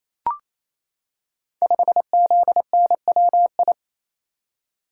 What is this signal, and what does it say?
Morse code, heard as a keyed tone, about 28 wpm, 700 Hz. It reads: 5ZNWI
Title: Code Groups - 5 Letter:Numbers - 28wpm